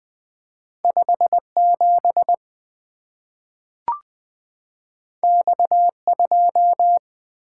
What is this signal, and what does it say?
Morse code, heard as a keyed tone, about 20 wpm, 700 Hz. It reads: 5 7 X 2
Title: Two Letter-Number Combinations - 20wpm